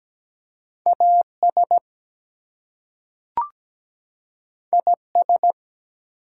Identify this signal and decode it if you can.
Morse code, heard as a keyed tone, about 17 wpm, 700 Hz. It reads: AS IS
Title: Two Letter Words - 17wpm